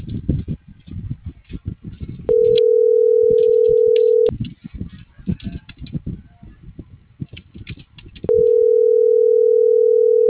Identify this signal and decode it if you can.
Ringback tone (North America), 440 + 480 Hz.